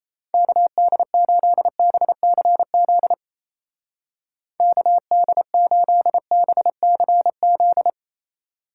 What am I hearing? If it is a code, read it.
Morse code, heard as a keyed tone, about 31 wpm, 700 Hz. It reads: KD8BCZ KD8BCZ
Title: Speed Racing - USA Call Signs 28wpm